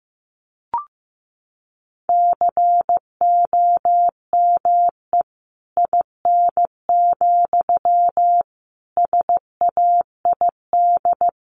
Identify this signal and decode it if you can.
Morse code, heard as a keyed tone, about 14 wpm, 700 Hz. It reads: COME IN, SAID
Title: Sentences from Top 1500 Words 15wpm